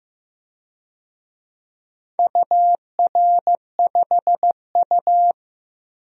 Morse code, heard as a keyed tone, about 15 wpm, 700 Hz. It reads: UR5U
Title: DXCC-100 Call Signs - 4 Characters - Repeated 3x - 15wpm